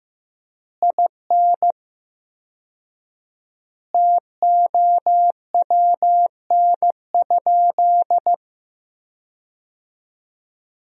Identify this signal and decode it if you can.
Morse code, heard as a keyed tone, about 15 wpm, 700 Hz. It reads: IN TOWN?
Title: 4x Word-Spacing - Sentences from Top 500 Words - 15wpm